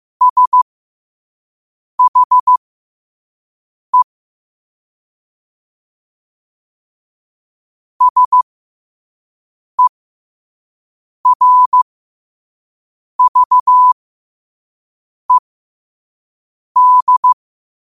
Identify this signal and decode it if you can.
Morse code, heard as a keyed tone, about 15 wpm, spaced as for 5 wpm, 1000 Hz. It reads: SHE SERVED
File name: qotdfiles / 5wpm